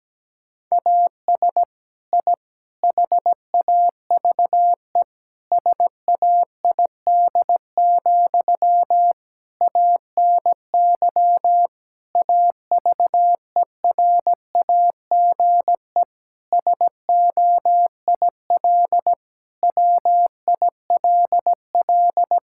Morse code, heard as a keyed tone, about 17 wpm, 700 Hz. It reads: AS I HAVE SAID, ANY AVERAGE SOIL WILL
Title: Sentences from Top 1000 Words - Review Part 6 - 17wpm